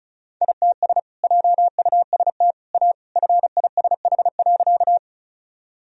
Morse code, heard as a keyed tone, about 36 wpm, 700 Hz. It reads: ITS JUST A FISH.
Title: Sentences from Top 1000 Words - Review Part 1 - 35wpm